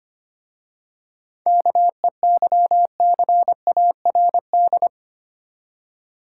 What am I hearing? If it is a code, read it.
Morse code, heard as a keyed tone, about 25 wpm, 690 Hz. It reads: KEYCARD
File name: practice / 25wpm / Easy Words with Hints - 25wpm